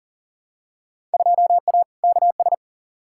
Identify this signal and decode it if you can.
Morse code, heard as a keyed tone, about 40 wpm, 700 Hz. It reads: 2A KS
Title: Field Day Exchanges - 40wpm